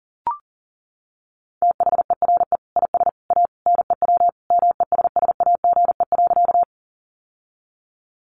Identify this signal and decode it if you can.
Morse code, heard as a keyed tone, about 37 wpm, 700 Hz. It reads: THERE IS A NEW MESSAGE.
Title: Sentences from Top 1100 Words 40wpm